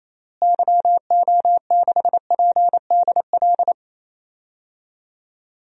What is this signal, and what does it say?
Morse code, heard as a keyed tone, about 28 wpm, 700 Hz. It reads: YO6PDL
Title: DXCC-100 Call Signs - 6 Characters - Repeated 2x - 28wpm